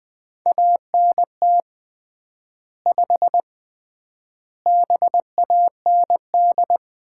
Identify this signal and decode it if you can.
Morse code, heard as a keyed tone, about 20 wpm, 700 Hz. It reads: ANT 5 BAND
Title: Common QSO Phrases - 3x Word-Spacing - 20wpm